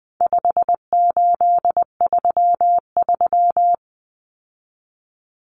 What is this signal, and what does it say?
Morse code, heard as a keyed tone, about 20 wpm, 700 Hz. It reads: 5833
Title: Numbers - 4-Digit 20wpm